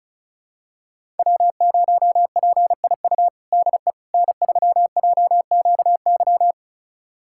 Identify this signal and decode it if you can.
Morse code, heard as a keyed tone, about 34 wpm, 700 Hz. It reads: W0PIU DE N3JQY
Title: USA Callsign DE Callsign - 5 Characters - 35wpm